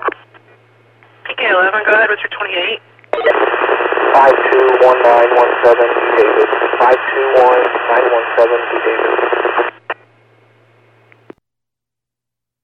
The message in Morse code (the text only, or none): none